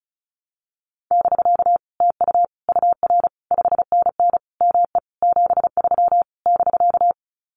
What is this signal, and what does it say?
Morse code, heard as a keyed tone, about 36 wpm, 700 Hz. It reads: <BK> TU UR 5NN ME 73 <BK>
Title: Common QSO Phrases Encore 35wpm